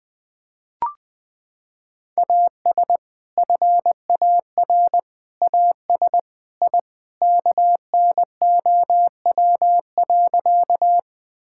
Morse code, heard as a keyed tone, about 21 wpm, 700 Hz. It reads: AS FAR AS I KNOW.
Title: Sentences with 3-Word Phrases - 20wpm